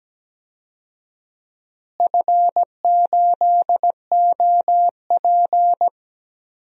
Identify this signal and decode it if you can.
Morse code, heard as a keyed tone, about 17 wpm, 700 Hz. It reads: F8OP
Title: DXCC-100 Call Signs - 4 Characters - Repeated 3x - 17wpm